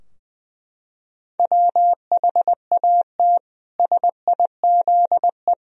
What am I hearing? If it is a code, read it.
Morse code, heard as a keyed tone, about 20 wpm, 700 Hz. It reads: WHAT SIZE
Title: Sentences from Top 1000 Words - Review Part 3 - 20wpm